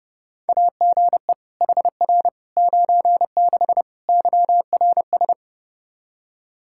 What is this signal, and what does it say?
Morse code, heard as a keyed tone, about 30 wpm, 700 Hz. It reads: AGE HR 96 YRS
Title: Practice QSO - Exchange 3 - More Info - 30wpm